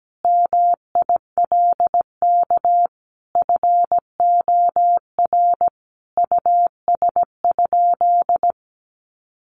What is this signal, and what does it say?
Morse code, heard as a keyed tone, about 17 wpm, 700 Hz. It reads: MILK FOR US?
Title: Sentences from Top 1500 Words 17wpm